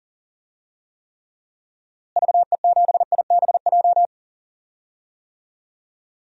Morse code, heard as a keyed tone, about 40 wpm, 700 Hz. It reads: VE7IBJ
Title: DXCC-100 Call Signs - 6 Characters - Repeated 3x - 40wpm